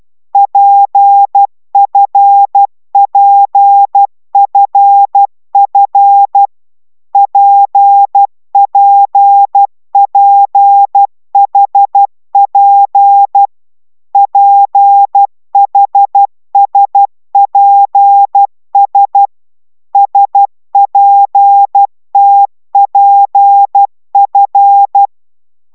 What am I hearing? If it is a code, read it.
Morse code, heard as a keyed tone, about 12 wpm, 800 Hz. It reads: PFPFF PPPHP PHSPS SPTPF
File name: taviro tanulas / cw 69 10 10